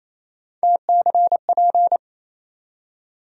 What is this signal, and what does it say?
Morse code, heard as a keyed tone, about 28 wpm, 700 Hz. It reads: TCP
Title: USA Call Sign Postfixes - 3 Letters 28wpm